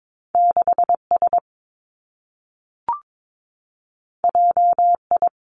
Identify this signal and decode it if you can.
Morse code, heard as a keyed tone, about 22 wpm, 700 Hz. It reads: 6 S J I
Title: Two Challenging Letter-Number Combinations 22wpm